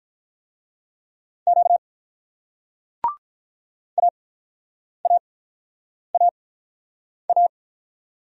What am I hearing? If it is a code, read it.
Morse code, heard as a keyed tone, about 45 wpm, spaced as for 9 wpm, 700 Hz. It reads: Q AAAA